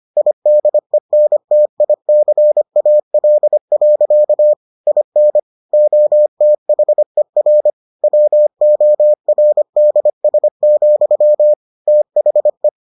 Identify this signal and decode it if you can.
Morse code, heard as a keyed tone, about 25 wpm, 600 Hz. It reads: IDENTICAL. IN OTHER WORDS, THE